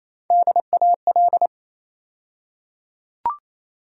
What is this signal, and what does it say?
Morse code, heard as a keyed tone, about 27 wpm, 700 Hz. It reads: DAL
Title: USA Call Sign Postfixes - 3 Letters 28wpm